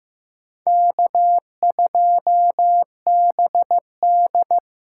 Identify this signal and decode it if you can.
Morse code, heard as a keyed tone, about 15 wpm, 700 Hz. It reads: K2BD
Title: USA Call Signs - 6x Character Spacing - 15wpm